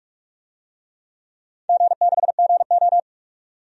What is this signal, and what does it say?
Morse code, heard as a keyed tone, about 41 wpm, 700 Hz. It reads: G6GO
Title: DXCC-100 Call Signs - 4 Characters - 45wpm